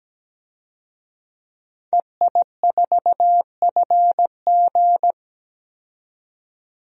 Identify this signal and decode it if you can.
Morse code, heard as a keyed tone, about 17 wpm, 700 Hz. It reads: EI4FG
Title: DXCC-100 Call Signs - 5 Characters - 17wpm